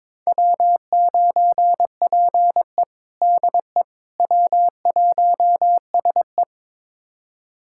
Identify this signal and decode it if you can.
Morse code, heard as a keyed tone, about 22 wpm, 700 Hz. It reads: W9PE DE W1SE
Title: USA Callsign DE Callsign - 4 Characters - 22wpm